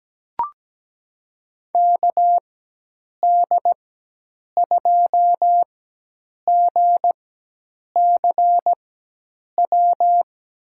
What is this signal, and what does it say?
Morse code, heard as a keyed tone, about 17 wpm, spaced as for 8 wpm, 700 Hz. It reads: KD2GCW
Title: USA Call Signs - 6 Characters - 4x Character Spacing - 17wpm